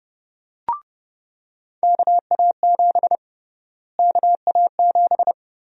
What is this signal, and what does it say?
Morse code, heard as a keyed tone, about 30 wpm, 700 Hz. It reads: KA7 KA7
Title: USA Call Sign Prefixes - Repeated 2x - 30wpm